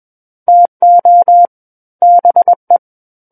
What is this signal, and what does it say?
Morse code, heard as a keyed tone, about 21 wpm, 700 Hz. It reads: TO BE
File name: Book One / 12-Chapter 12 - 21 WPM